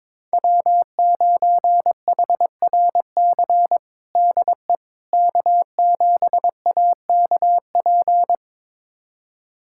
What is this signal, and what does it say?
Morse code, heard as a keyed tone, about 22 wpm, 700 Hz. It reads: W9HRC DE K7AKP
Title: USA Callsign DE Callsign - 5 Characters - 22wpm